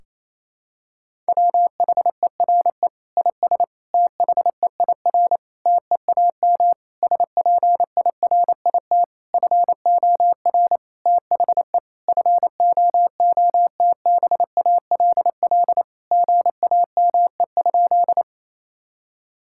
Morse code, heard as a keyed tone, about 28 wpm, 700 Hz. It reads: WHERE IS THEIR TEAM SPIRIT FOR THE FOOTBALL GAME?
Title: Sentences from Top 1600 Words 28wpm